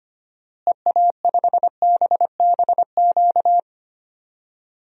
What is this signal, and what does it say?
Morse code, heard as a keyed tone, about 25 wpm, 700 Hz. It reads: EA5BBQ
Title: DXCC-100 Call Signs - 6 Characters - Repeated 2x - 25wpm